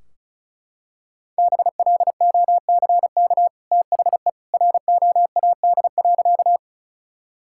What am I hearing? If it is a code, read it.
Morse code, heard as a keyed tone, about 34 wpm, 700 Hz. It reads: BLOCK THE ROAD.
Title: Sentences from Top 1600 Words 35wpm